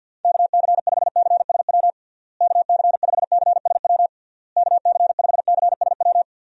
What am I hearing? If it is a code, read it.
Morse code, heard as a keyed tone, about 52 wpm, 700 Hz. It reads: KC5CSW KC5CSW KC5CSW
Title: USA Call Signs - 6 Characters - Repeated 3x - 50wpm